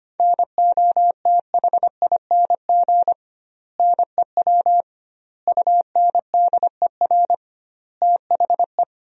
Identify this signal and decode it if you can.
Morse code, heard as a keyed tone, about 25 wpm, 700 Hz. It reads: NOTHING NEW UNDER THE
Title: 2x Word-Spacing - Sentences from Top 500 Words - 25wpm